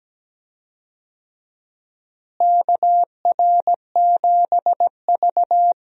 Morse code, heard as a keyed tone, about 17 wpm, 700 Hz. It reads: KR7V
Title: USA Call Signs - 4 Characters - Repeated 2x - 17wpm